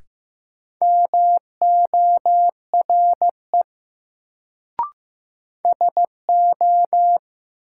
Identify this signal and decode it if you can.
Morse code, heard as a keyed tone, about 15 wpm, 700 Hz. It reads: MORE SO
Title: Sentences from Top 100 Words - New Word Drill - 15wpm